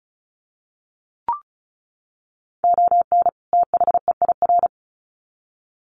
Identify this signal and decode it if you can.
Morse code, heard as a keyed tone, about 34 wpm, 700 Hz. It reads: ON THEIR
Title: Sets of 2 Words 35wpm